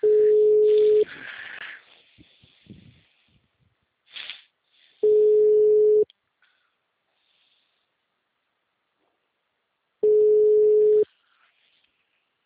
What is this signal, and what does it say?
Ringback tone (Europe, 425 Hz).
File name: out- -2007- - - .11726